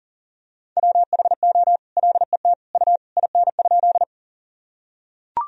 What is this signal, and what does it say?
Morse code, heard as a keyed tone, about 40 wpm, 700 Hz. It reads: WHO LET U IN?
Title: Sentences from Top 300 Words and Abbreviations 40wpm